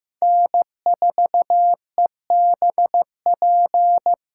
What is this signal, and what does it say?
Morse code, heard as a keyed tone, about 15 wpm, 700 Hz. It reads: N4EBP
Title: USA Call Signs - 6x Character Spacing - 15wpm